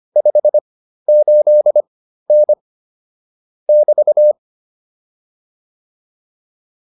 Morse code, heard as a keyed tone, about 25 wpm, spaced as for 13 wpm, 600 Hz. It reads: 58N <BT>